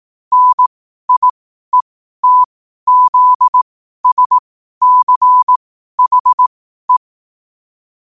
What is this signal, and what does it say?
Morse code, heard as a keyed tone, about 18 wpm, spaced as for 12 wpm, 1000 Hz. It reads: NIETZSCHE
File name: qotdfiles / 12wpm